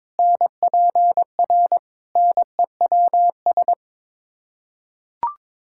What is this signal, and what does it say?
Morse code, heard as a keyed tone, about 21 wpm, 700 Hz. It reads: NPR NEWS
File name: practice / 22wpm / Build A Phrase - 5-Word Phrases - 22wpm